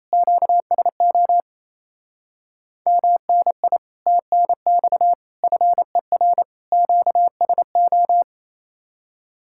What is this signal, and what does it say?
Morse code, heard as a keyed tone, about 32 wpm, 700 Hz. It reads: QSO MNI TNX FER QSO
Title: Speed-Racing Common QSO Phrases Encore - 28wpm